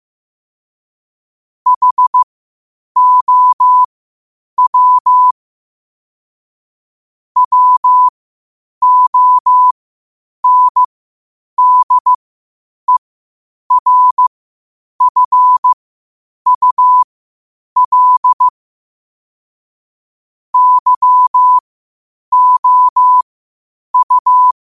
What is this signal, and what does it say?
Morse code, heard as a keyed tone, about 15 wpm, spaced as for 8 wpm, 1000 Hz. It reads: HOW WONDERFUL YOU
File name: qotdfiles / 8wpm